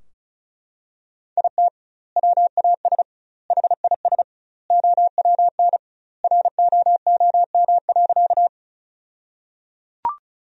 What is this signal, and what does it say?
Morse code, heard as a keyed tone, about 36 wpm, 700 Hz. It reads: IT WAS HIS OWN ROOM.
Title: 2x Word-Spacing - Sentences from Top 300 Words - 35wpm